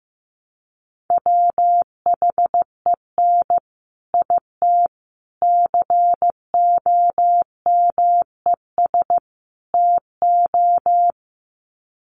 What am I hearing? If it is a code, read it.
Morse code, heard as a keyed tone, about 15 wpm, 700 Hz. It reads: WHEN IT COMES TO
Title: Sets of 4 Words 15wpm